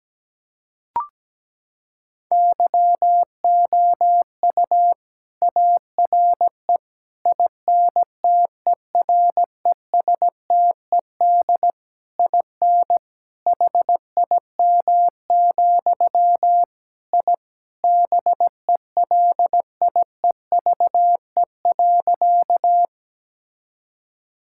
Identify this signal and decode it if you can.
Morse code, heard as a keyed tone, about 17 wpm, 700 Hz. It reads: YOU ARE INTERESTED IN HIM, I BELIEVE.
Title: Sentences from Top 1000 Words - Review Part 6 - 17wpm